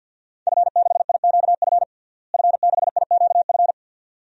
Morse code, heard as a keyed tone, about 52 wpm, 700 Hz. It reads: F6IQF F6IQF
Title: DXCC-100 Call Signs - 5 Characters - Repeated 2x - 50wpm